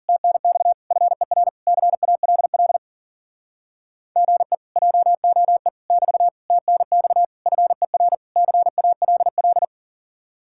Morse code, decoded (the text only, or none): TNX FER CALL GE JOE <BT> TNX FER CALL